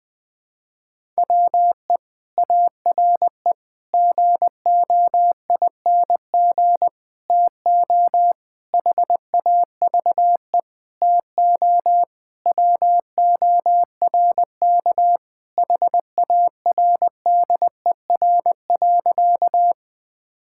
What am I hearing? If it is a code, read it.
Morse code, heard as a keyed tone, about 20 wpm, 700 Hz. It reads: WE ARE GOING TO HAVE TO WORK HARDER.
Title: Sentences with 6-Word Phrases - 20wpm